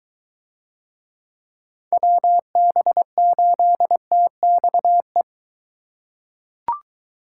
Morse code, heard as a keyed tone, about 23 wpm, 700 Hz. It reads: WB8TXE